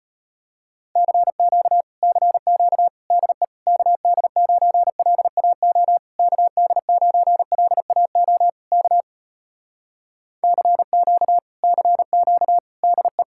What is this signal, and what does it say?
Morse code, heard as a keyed tone, about 34 wpm, 700 Hz. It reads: CQ CQ DE KD9LAO KD9LAO K CQ CQ DE